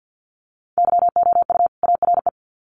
Morse code, heard as a keyed tone, about 50 wpm, 700 Hz. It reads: YOU ARE
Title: Sets of 2 Words 50wpm